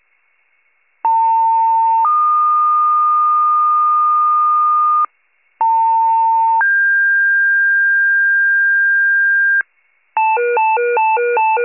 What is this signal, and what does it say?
Two-tone page, heard as a one-second tone, then a three-second tone, about 900 then 1220 Hz, about 900 then 1600 Hz, with noise. It is heard from 1.0 to 9.6 s.
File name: pager / 78 RESCUE / wcalert